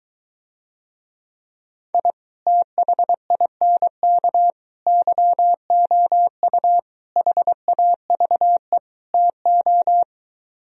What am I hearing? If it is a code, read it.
Morse code, heard as a keyed tone, about 23 wpm, 700 Hz. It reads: I THINK YOU HAVE TO